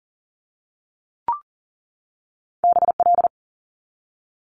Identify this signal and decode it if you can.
Morse code, heard as a keyed tone, about 40 wpm, 700 Hz. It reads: BL
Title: Multi-letter Phonemes 40wpm